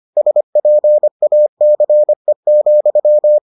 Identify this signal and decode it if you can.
Morse code, heard as a keyed tone, about 25 wpm, 600 Hz. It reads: SPACE,